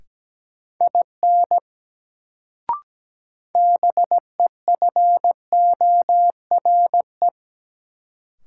Morse code, heard as a keyed tone, about 17 wpm, 700 Hz. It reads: IN BEFORE
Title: Sets of 2 Words - New Word Drill - 17wpm